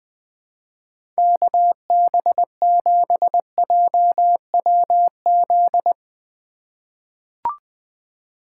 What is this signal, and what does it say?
Morse code, heard as a keyed tone, about 20 wpm, 700 Hz. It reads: KB7JWZ
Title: USA Call Signs - 6 Characters - 3x Character Spacing - 20wpm